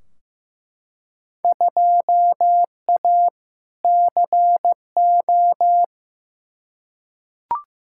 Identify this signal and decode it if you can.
Morse code, heard as a keyed tone, about 15 wpm, 700 Hz. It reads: 2A CO
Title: Field Day Exchanges - 15wpm